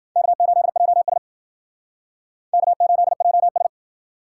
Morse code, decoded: K8JS K8JS

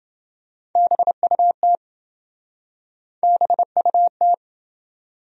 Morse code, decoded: BUT BUT

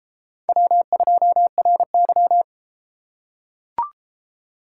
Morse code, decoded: W2RY

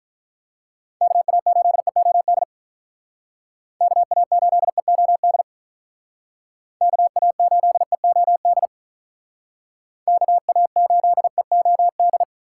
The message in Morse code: KA8EOD KA8EOD KA8EOD KA8EOD

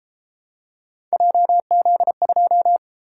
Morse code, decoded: JZ2

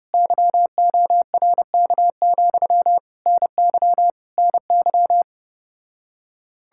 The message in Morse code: YORK, NY NY

